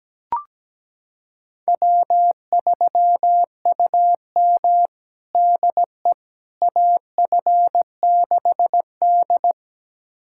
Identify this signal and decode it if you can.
Morse code, heard as a keyed tone, about 17 wpm, 700 Hz. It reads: W3UM DE AF6D
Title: USA Callsign DE Callsign - 4 Characters - 17wpm